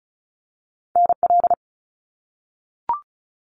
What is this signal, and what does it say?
Morse code, heard as a keyed tone, about 36 wpm, 700 Hz. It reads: NL
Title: Canadian Provinces and Territories - Repeated 3x - 35wpm